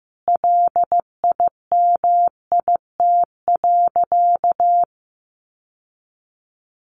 Morse code, heard as a keyed tone, about 15 wpm, 700 Hz. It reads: LIMIT.
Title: Sentences from Top 1500 Words 15wpm